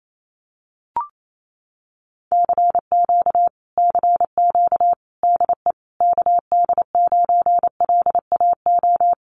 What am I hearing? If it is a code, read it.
Morse code, heard as a keyed tone, about 28 wpm, 700 Hz. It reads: CQ CQ DE KD9LAO
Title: Common QSO Phrases Encore 28wpm